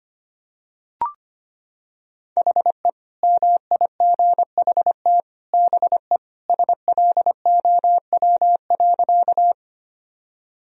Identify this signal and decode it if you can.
Morse code, heard as a keyed tone, about 25 wpm, 700 Hz. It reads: HE MIGHT BE SLOW.
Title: Sentences from Top 1000 Words - Review Part 2 - 25wpm